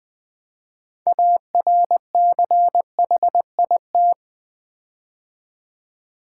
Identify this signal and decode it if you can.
Morse code, heard as a keyed tone, about 20 wpm, 700 Hz. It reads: ARCHIT
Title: Build A Word - 6 Letters At A Time - 12-Letter Words - 20wpm